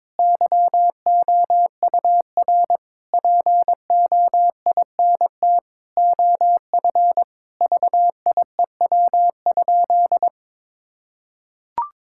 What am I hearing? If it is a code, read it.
Morse code, heard as a keyed tone, about 22 wpm, 700 Hz. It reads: YOUR POINT OF VIEW?
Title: Sentences from Top 1000 Words - Review Part 6 - 22wpm